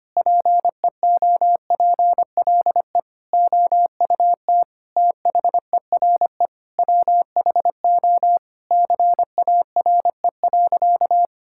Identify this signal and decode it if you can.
Morse code, heard as a keyed tone, about 25 wpm, 700 Hz. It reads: PEOPLE OUT THERE WHO CARE.
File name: practice / 25wpm / Sentences with 9-Word Phrases - 25wpm